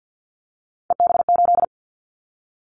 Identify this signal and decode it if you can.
Morse code, heard as a keyed tone, about 50 wpm, 700 Hz. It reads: EB8
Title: Callsign Prefixes for Top 100 DXCC - 50wpm